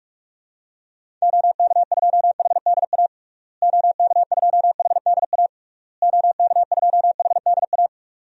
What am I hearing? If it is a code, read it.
Morse code, heard as a keyed tone, about 45 wpm, 700 Hz. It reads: OK2HDA OK2HDA OK2HDA
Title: DXCC-100 Call Signs - 6 Characters - Repeated 3x - 45wpm